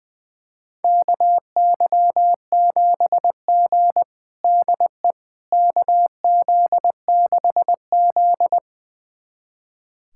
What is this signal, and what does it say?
Morse code, heard as a keyed tone, about 20 wpm, 700 Hz. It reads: KY7G DE KZ6Z
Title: USA Callsign DE Callsign - 4 Characters - 20wpm